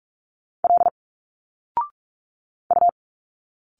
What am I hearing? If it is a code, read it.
Morse code, heard as a keyed tone, about 46 wpm, 700 Hz. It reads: L U
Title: Single Letters 45wpm